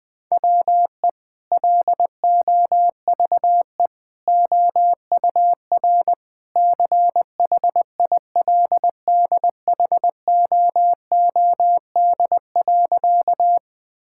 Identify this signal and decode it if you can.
Morse code, heard as a keyed tone, about 21 wpm, 700 Hz. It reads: WE LOVE OUR CHILDHOOD.
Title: Sentences with 2-Word Phrases - Spoken Beginning - 20wpm